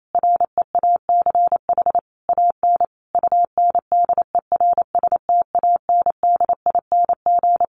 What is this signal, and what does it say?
Morse code, heard as a keyed tone, about 28 wpm, 700 Hz. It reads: REACH AN UNDERSTANDING
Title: Sentences from Top 1500 Words 28wpm